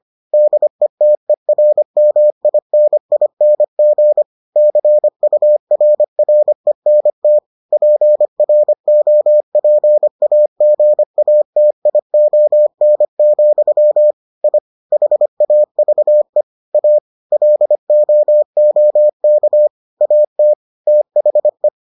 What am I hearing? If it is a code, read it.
Morse code, heard as a keyed tone, about 25 wpm, 600 Hz. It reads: DETERMINING CURRENT PROPAGATION, I HAVE A LOOK AT THE